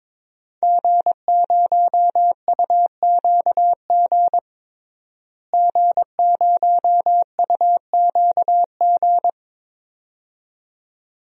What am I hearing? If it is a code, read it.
Morse code, heard as a keyed tone, about 22 wpm, 700 Hz. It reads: G0UQG G0UQG
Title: DXCC-100 Call Signs - 5 Characters - Repeated 2x - 22wpm